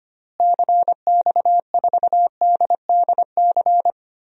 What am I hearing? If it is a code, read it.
Morse code, heard as a keyed tone, about 25 wpm, 700 Hz. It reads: CX4DDC
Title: DXCC-100 Call Signs - 6 Characters - Repeated 3x - 25wpm